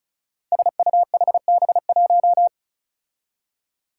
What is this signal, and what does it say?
Morse code, heard as a keyed tone, about 34 wpm, 700 Hz. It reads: SUHB1
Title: Five Challenging Letter-Number Combinations 35wpm